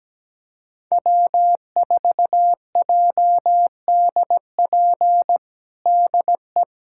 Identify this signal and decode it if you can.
Morse code, heard as a keyed tone, about 17 wpm, 700 Hz. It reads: W4JDP DE
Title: USA Callsign DE Callsign 17wpm